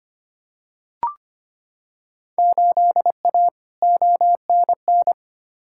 Morse code, heard as a keyed tone, about 25 wpm, 700 Hz. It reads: 8A ONN
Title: Field Day Exchanges Encore - 25wpm